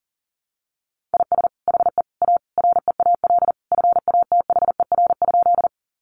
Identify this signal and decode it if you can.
Morse code, heard as a keyed tone, about 37 wpm, 700 Hz. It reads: IS HE A REAL FATHER?
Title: Sentences from Top 300 Words 40wpm